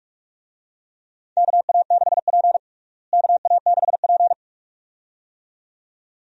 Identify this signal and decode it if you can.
Morse code, heard as a keyed tone, about 46 wpm, 700 Hz. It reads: KA6P KA6P
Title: USA Call Signs - 4 Characters - Repeated 2x - 45wpm